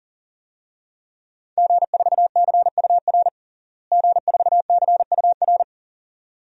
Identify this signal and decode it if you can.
Morse code, heard as a keyed tone, about 37 wpm, 700 Hz. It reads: G4CUR G4CUR
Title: DXCC-100 Call Signs - 5 Characters - Repeated 2x - 40wpm